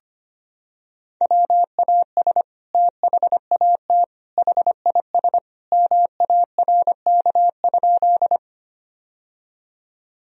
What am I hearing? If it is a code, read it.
Morse code, heard as a keyed tone, about 25 wpm, 700 Hz. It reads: WAS THAT HIS MARK?
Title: Sentences from Top 1000 Words - Review Part 1 - 25wpm